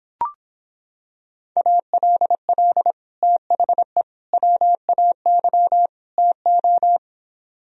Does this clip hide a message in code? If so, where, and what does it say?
Yes: Morse code from 1.6 to 7.0 s, heard as a keyed tone, about 26 wpm, 700 Hz. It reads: ALL THE WAY TO